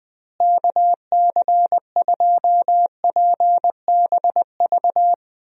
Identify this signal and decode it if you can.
Morse code, heard as a keyed tone, about 20 wpm, 700 Hz. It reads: KC2PBV
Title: USA Call Signs - 6 Characters - Repeated 3x - 20wpm